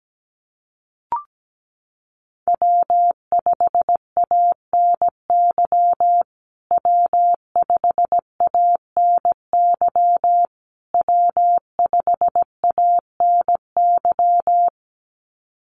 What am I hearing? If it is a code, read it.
Morse code, heard as a keyed tone, about 17 wpm, 700 Hz. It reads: W5ANY W5ANY W5ANY
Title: USA Call Signs Repeated 3x 17wpm